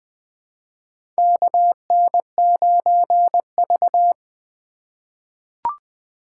Morse code, heard as a keyed tone, about 20 wpm, 700 Hz. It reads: KN9V
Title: USA Call Signs - 4 Characters - 2x Character Spacing - 20wpm